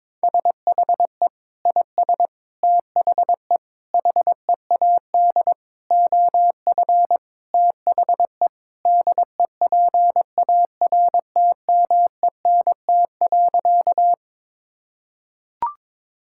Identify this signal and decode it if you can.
Morse code, heard as a keyed tone, about 22 wpm, 700 Hz. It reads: SHE IS THE HEAD OF THE DEPARTMENT.
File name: practice / 22wpm / Sentences with 3-Word Phrases - 22wpm